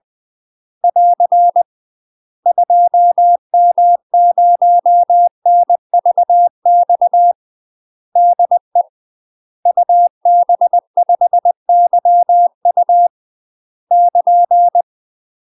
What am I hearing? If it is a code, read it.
Morse code, heard as a keyed tone, about 20 wpm, 700 Hz. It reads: <AR> 2M0NVX DE UB5YU <KN>
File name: UK MASTER QSOs TRAINING / 20WPM / UK MASTER QSOs 0052